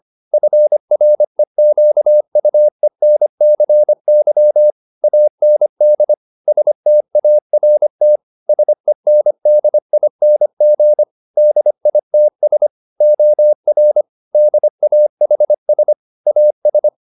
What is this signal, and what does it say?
Morse code, heard as a keyed tone, about 25 wpm, 600 Hz. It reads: FREQUENCY AND START SENDING DITS OR DAHS AS